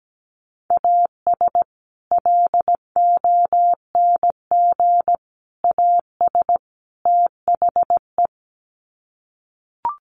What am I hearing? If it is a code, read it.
Morse code, heard as a keyed tone, about 17 wpm, 700 Hz. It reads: AS LONG AS THE
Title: Sets of 4 Words 17wpm